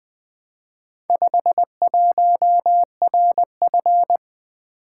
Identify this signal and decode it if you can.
Morse code, heard as a keyed tone, about 20 wpm, 700 Hz. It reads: 51RF
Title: Four Letter-Number Combinations - 20wpm